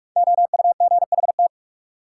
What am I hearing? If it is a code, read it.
Morse code, heard as a keyed tone, about 46 wpm, 700 Hz. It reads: OUGHT
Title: Five Letter Words - 45wpm